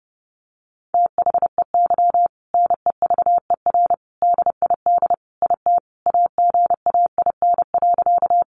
Morse code, heard as a keyed tone, about 30 wpm, 700 Hz. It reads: THEY NEVER DID IT AGAIN.
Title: Sentences from Top 300 Words 30wpm